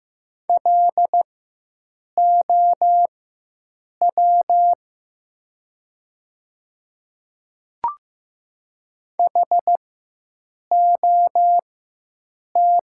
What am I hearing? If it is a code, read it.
Morse code, heard as a keyed tone, about 15 wpm, spaced as for 7 wpm, 700 Hz. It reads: LOW HOT
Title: 4x Character Spacing - Top 500 Words - 15wpm